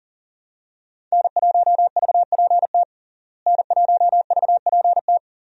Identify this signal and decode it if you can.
Morse code, heard as a keyed tone, about 40 wpm, 700 Hz. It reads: N1VPT N1VPT
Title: USA Call Signs - 5 Characters - Repeated 3x - 40wpm